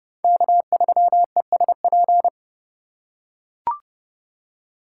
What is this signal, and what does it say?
Morse code, heard as a keyed tone, about 30 wpm, 700 Hz. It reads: K3ESP